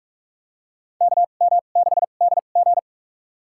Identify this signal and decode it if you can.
Morse code, heard as a keyed tone, about 46 wpm, spaced as for 32 wpm, 700 Hz. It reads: KM6DG
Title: USA Call Signs - 5 Characters - 2x Character Spacing - 45wpm